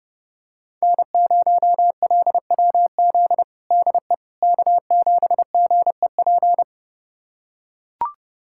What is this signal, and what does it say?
Morse code, heard as a keyed tone, about 30 wpm, 700 Hz. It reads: N0LWZ DE K7GEP
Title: USA Callsign DE Callsign - 5 Characters - 30wpm